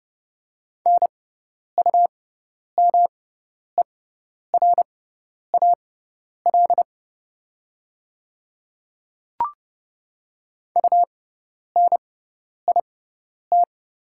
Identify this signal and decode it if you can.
Morse code, heard as a keyed tone, about 30 wpm, spaced as for 10 wpm, 700 Hz. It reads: NUMERAL UNIT
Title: 6x Character Spacing - Top 500 Words - 30wpm